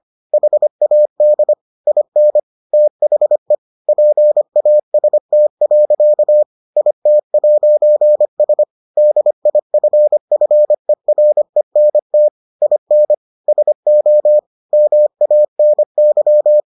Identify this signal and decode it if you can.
Morse code, heard as a keyed tone, about 25 wpm, 600 Hz. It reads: HAD IN THE PAST. IT'S DIFFERENT IN SO MANY